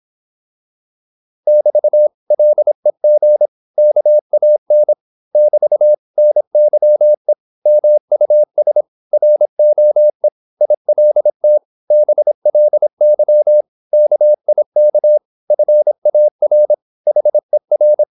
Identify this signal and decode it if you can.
Morse code, heard as a keyed tone, about 26 wpm, 600 Hz. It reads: <BT> LEG KAN <BT> NYE MUS ROE ILT BLY KIK FAR HER
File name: Korte ord DK 26wpm